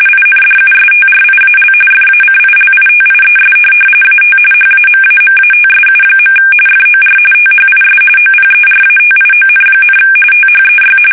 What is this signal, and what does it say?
RTTY, about 45.45 baud. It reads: INDUSTRIAL FACILITIES AT HAMAMATSU. FOR THIS OPERATION, THEY
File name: fsk850 / USS-Massachusetts